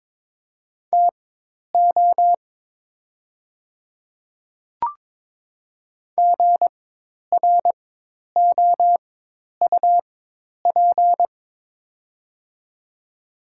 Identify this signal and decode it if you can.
Morse code, heard as a keyed tone, about 22 wpm, spaced as for 10 wpm, 700 Hz. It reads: TO GROUP
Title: 4x Character Spacing - Top 300 Words - 22wpm